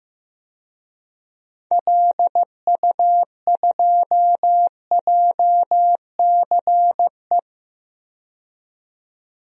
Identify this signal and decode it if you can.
Morse code, heard as a keyed tone, about 15 wpm, 700 Hz. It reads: LU2JCE
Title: DXCC-100 Call Signs - 6 Characters - 15wpm